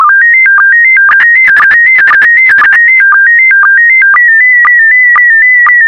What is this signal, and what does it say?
SSTV (slow-scan television), PD 120.